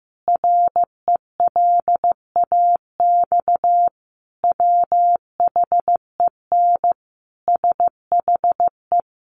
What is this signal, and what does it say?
Morse code, heard as a keyed tone, about 15 wpm, 700 Hz. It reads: RELAX WHEN SHE
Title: Sentences from Top 1400 Words 15wpm